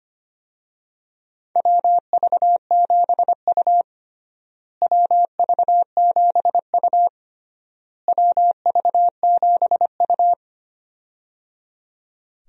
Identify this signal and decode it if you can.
Morse code, heard as a keyed tone, about 25 wpm, 700 Hz. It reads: WV7U WV7U WV7U
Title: USA Call Signs - 4 Characters - Repeated 3x - 25wpm